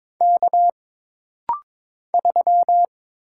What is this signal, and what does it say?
Morse code, heard as a keyed tone, about 22 wpm, 700 Hz. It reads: K 3
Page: This rapid-fire timing is geared to push your ICR (Instant Character Recognition) skills.